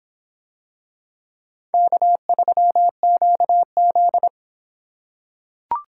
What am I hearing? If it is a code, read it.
Morse code, heard as a keyed tone, about 26 wpm, 700 Hz. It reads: K3QZ